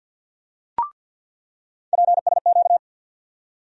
Morse code, heard as a keyed tone, about 49 wpm, 700 Hz. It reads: PSQ
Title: USA Call Sign Postfixes - 3 Letters 50wpm